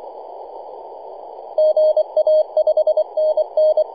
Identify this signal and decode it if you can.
Morse code, heard as a keyed tone, about 24 wpm, 620 Hz. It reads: GA 5NN
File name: e51dwc-21-010mhz-cw